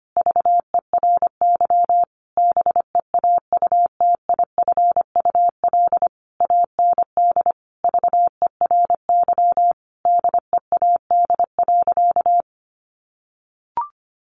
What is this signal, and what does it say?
Morse code, heard as a keyed tone, about 25 wpm, 700 Hz. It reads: VERY BEAUTIFUL AND VERY DEAD.
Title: A Sound Of Thunder 25wpm